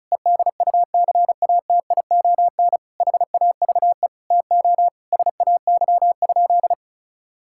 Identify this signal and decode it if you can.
Morse code, heard as a keyed tone, about 34 wpm, 700 Hz. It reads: EDUCATION HAVE TO SAY?
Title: Sentences from Top 1000 Words - Review Part 6 - 35wpm